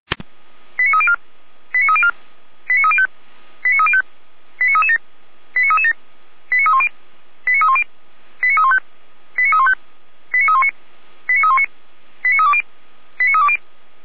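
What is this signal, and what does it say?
Five-tone selective call (ZVEI), heard as quick runs of five tones, tones of about 70 ms. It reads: 89304 89304 89305 89305 89337 89337 89310 89310 89315 89315 89319 89319 89320 89320
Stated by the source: ZVEI five-tone call 89280 received by SDR.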